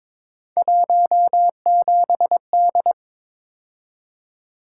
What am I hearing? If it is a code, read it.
Morse code, heard as a keyed tone, about 22 wpm, 700 Hz. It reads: 17D
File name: practice / 22wpm / Three Letter-Number Combinations - 22wpm